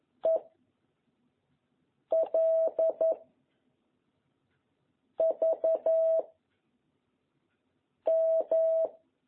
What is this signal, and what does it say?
Morse code, heard as a keyed tone, about 11 wpm, 660 Hz. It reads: E L V M
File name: jh8cbh / 2021cw